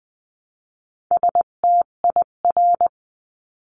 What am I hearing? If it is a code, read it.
Morse code, heard as a keyed tone, about 20 wpm, 700 Hz. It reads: STIR
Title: 258 Lesson 42c - Words 16-20wpm